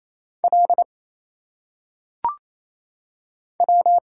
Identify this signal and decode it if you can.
Morse code, heard as a keyed tone, about 27 wpm, 700 Hz. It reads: L W
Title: Single Letters - Phonetic Pronunciation 28wpm